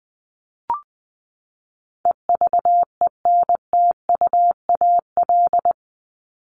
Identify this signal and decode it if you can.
Morse code, heard as a keyed tone, about 20 wpm, 700 Hz. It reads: EVENTUAL
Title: LY Suffix - Top 100 Words with Base Words 20wpm